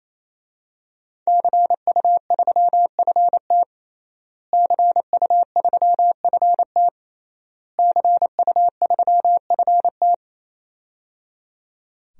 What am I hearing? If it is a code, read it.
Morse code, heard as a keyed tone, about 28 wpm, 700 Hz. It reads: CU3FT CU3FT CU3FT
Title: DXCC-100 Call Signs - 5 Characters - Repeated 3x - 28wpm